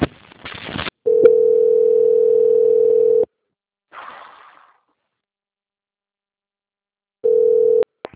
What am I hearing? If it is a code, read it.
Ringback tone (North America), 440 + 480 Hz.